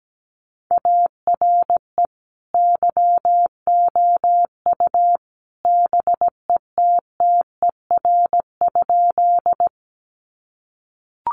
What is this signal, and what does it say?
Morse code, heard as a keyed tone, about 17 wpm, 700 Hz. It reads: ARE YOU BETTER?
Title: Sentences from Top 400 Words 17wpm